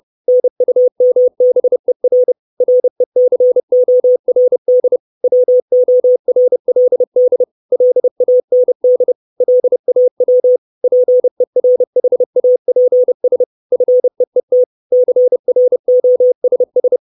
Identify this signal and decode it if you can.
Morse code, heard as a keyed tone, about 30 wpm, 500 Hz. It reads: NUMBER RECORD WORLD LAND LAW PERHAPS FEET CROSS
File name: 632 Words 30WPM